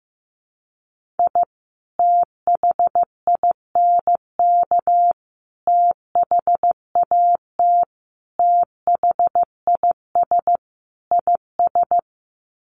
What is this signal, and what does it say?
Morse code, heard as a keyed tone, about 15 wpm, 700 Hz. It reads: I THINK THAT THIS IS
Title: Sets of 5 Words Encore2 15wpm